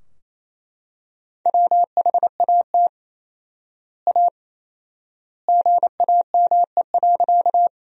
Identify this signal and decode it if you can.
Morse code, heard as a keyed tone, about 28 wpm, 700 Hz. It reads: WHAT A GAME.
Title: 4x Word-Spacing - Sentences from Top 500 Words - 28wpm